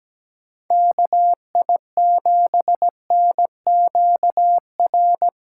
Morse code, heard as a keyed tone, about 17 wpm, 700 Hz. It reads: KI7NQR
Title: USA Call Signs - 6 Characters - 8x Character Spacing - 17wpm